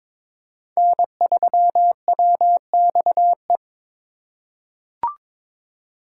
Morse code, heard as a keyed tone, about 22 wpm, 700 Hz. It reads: N3WXE
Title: USA Call Signs - 5 Characters - 3x Character Spacing - 22wpm